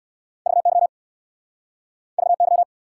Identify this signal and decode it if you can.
Morse code, heard as a keyed tone, about 69 wpm, 700 Hz. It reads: VY VY